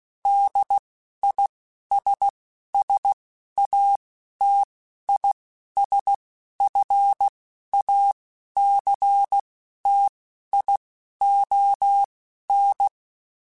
Morse code, heard as a keyed tone, about 16 wpm, spaced as for 11 wpm, 800 Hz. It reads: DISSATISFACTION
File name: General / General-mp3s / ISS @10